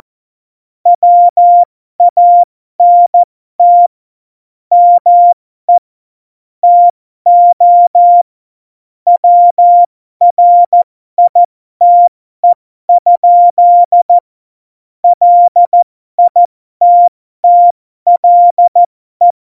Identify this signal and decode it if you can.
Morse code, heard as a keyed tone, about 14 wpm, spaced as for 12 wpm, 700 Hz. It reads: WANT ME TO WRITE? LITTLE
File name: Morsekurs / Witze